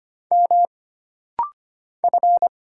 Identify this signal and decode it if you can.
Morse code, heard as a keyed tone, about 25 wpm, 700 Hz. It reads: M F
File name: Single Character (Letter:Number:Punctuation:Prosign) - Rapid-Fire - 25wpm